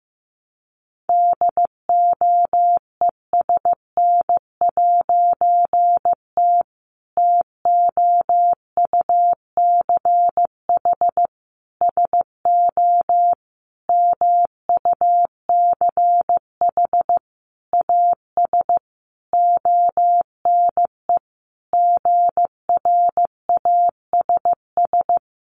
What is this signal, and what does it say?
Morse code, heard as a keyed tone, about 15 wpm, 700 Hz. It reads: DOESN'T TOUCH SO MUCH AS ONE GRASS
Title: A Sound Of Thunder 15wpm